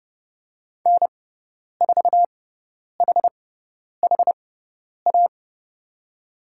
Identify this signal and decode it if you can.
Morse code, heard as a keyed tone, about 30 wpm, spaced as for 10 wpm, 700 Hz. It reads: N4HHA
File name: Lesson 18g - Letter m - Call Signs 30wpm